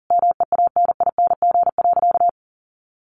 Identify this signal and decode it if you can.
Morse code, heard as a keyed tone, about 40 wpm, 700 Hz. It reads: MEANING.
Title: Sentences from Top 1100 Words 40wpm